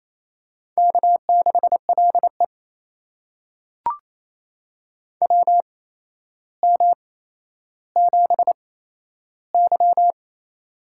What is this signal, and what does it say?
Morse code, heard as a keyed tone, about 28 wpm, spaced as for 12 wpm, 700 Hz. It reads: K6LE WM7Y
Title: USA Call Signs - 4 Characters - 8x Character Spacing - 28wpm